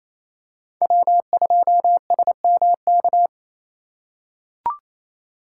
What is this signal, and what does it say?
Morse code, heard as a keyed tone, about 28 wpm, 700 Hz. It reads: W2SMK
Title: USA Call Signs - 3x Character Spacing - 28wpm